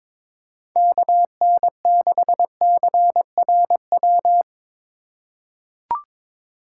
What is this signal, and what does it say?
Morse code, heard as a keyed tone, about 22 wpm, 700 Hz. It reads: KN6CRW
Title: USA Call Signs - 6 Characters - 4x Character Spacing - 22wpm